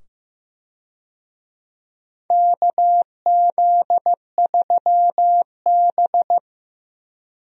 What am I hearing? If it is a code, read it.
Morse code, heard as a keyed tone, about 15 wpm, 700 Hz. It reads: KZ3B
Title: USA Call Signs - 4 Characters - 15wpm